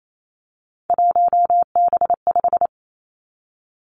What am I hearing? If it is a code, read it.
Morse code, heard as a keyed tone, about 28 wpm, 700 Hz. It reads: 1B5
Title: Three Challenging Letter-Number Combinations 28wpm